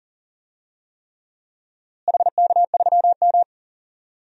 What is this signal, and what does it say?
Morse code, heard as a keyed tone, about 37 wpm, 700 Hz. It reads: HK3M
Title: DXCC-100 Call Signs - 4 Characters - Repeated 3x - 40wpm